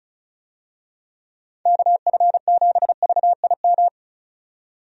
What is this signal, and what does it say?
Morse code, heard as a keyed tone, about 36 wpm, 700 Hz. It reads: KF7VIM
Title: USA Call Signs - 6 Characters - 35wpm